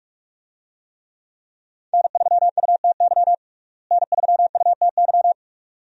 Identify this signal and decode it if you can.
Morse code, heard as a keyed tone, about 42 wpm, 700 Hz. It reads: N3UTY N3UTY
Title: USA Call Signs - 5 Characters - Repeated 2x - 45wpm